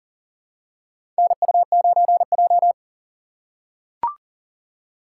Morse code, heard as a keyed tone, about 40 wpm, 700 Hz. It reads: NU9J
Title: USA Call Signs - 4 Characters - 4x Character Spacing - 40wpm